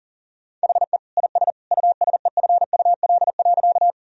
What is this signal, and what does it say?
Morse code, heard as a keyed tone, about 40 wpm, 700 Hz. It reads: HE IS USEFUL.
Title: Sentences from Top 1000 Words - Review Part 4 - 40wpm